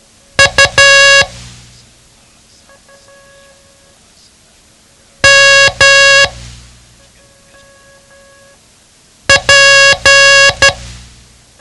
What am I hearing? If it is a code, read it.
Morse code, heard as a keyed tone, about 12 wpm, 1650 Hz. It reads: U M P
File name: ham radio / CodeQuick / Code Quick 07